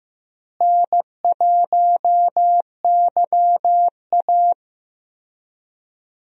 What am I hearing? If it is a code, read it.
Morse code, heard as a keyed tone, about 15 wpm, 700 Hz. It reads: N1YA
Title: USA Call Signs - 4 Characters - Repeated 3x - 15wpm